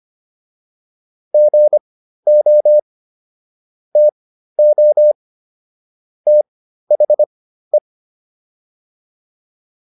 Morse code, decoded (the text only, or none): GO TO THE